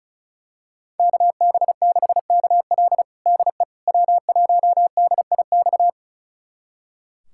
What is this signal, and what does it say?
Morse code, heard as a keyed tone, about 36 wpm, 700 Hz. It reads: KB6KL DE W1DIX
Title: USA Callsign DE Callsign 35wpm